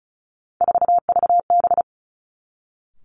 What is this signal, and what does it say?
Morse code, heard as a keyed tone, about 36 wpm, 700 Hz. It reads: 4VB
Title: Three Challenging Letter-Number Combinations 35wpm